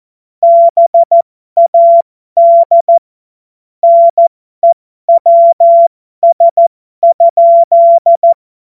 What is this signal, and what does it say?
Morse code, heard as a keyed tone, about 14 wpm, spaced as for 12 wpm, 680 Hz. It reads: BAD NEWS?
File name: Morsekurs / Witze